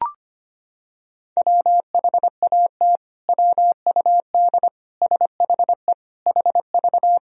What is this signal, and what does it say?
Morse code, heard as a keyed tone, about 25 wpm, 700 Hz. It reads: WHAT WUD SHE HV
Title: Sentences from Top 100 Words and Abbreviations 25wpm